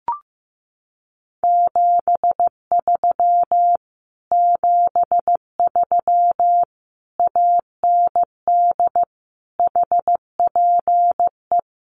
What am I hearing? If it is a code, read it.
Morse code, heard as a keyed tone, about 15 wpm, 700 Hz. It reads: 73 73 AND HPE
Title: Common QSO Phrases Encore 15wpm